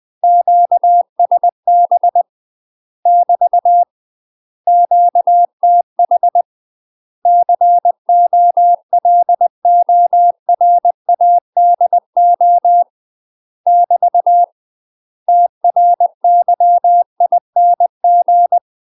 Morse code, heard as a keyed tone, about 20 wpm, 700 Hz. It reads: QSB <BT> QTH COLORADO <BT> TRYING
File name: UK MASTER QSOs TRAINING / 20WPM / UK MASTER QSOs 0050